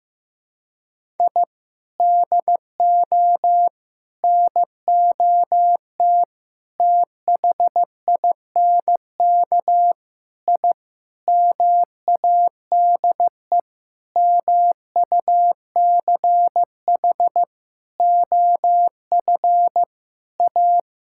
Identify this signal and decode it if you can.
Morse code, heard as a keyed tone, about 15 wpm, 700 Hz. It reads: I DO NOT THINK I MADE MUCH OF A
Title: Sentences from Top 1000 Words - Review Part 6 - 15wpm